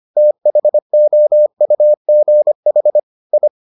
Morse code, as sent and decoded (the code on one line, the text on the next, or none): - .... --- ..- --. .... / ..
THOUGH I